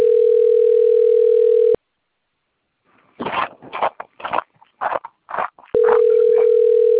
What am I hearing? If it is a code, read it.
Ringback tone (North America), 440 + 480 Hz.